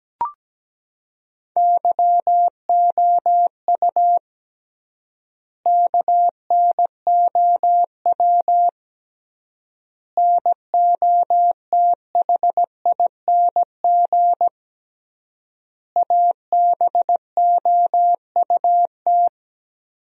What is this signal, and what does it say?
Morse code, heard as a keyed tone, about 17 wpm, 700 Hz. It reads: YOU KNOW NOTHING ABOUT
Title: 3x Word-Spacing - Sentences from Top 500 Words - 17wpm